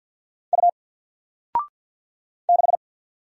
Morse code, heard as a keyed tone, about 50 wpm, 700 Hz. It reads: U 6
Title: Single Letter-Number 50wpm